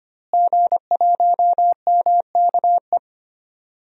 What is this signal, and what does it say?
Morse code, heard as a keyed tone, about 25 wpm, 700 Hz. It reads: G1MKE
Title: DXCC-100 Call Signs - 5 Characters - Repeated 3x - 25wpm